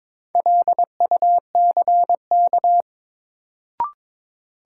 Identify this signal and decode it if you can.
Morse code, heard as a keyed tone, about 22 wpm, 700 Hz. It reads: LUCK
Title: Sentences from Top 1500 Words - New Word Drill - 22wpm